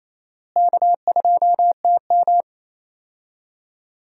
Morse code, heard as a keyed tone, about 28 wpm, 700 Hz. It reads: K2TM
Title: USA Call Signs - 4 Characters - Repeated 2x - 28wpm